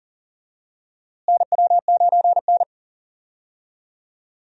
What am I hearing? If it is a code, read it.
Morse code, heard as a keyed tone, about 40 wpm, 700 Hz. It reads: NW9N
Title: USA Call Signs - 4 Characters - Repeated 3x - 40wpm